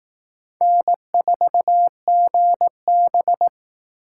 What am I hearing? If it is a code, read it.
Morse code, heard as a keyed tone, about 18 wpm, 700 Hz. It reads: N4GB
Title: Speed Racing - USA 1x2 Call Signs 15wpm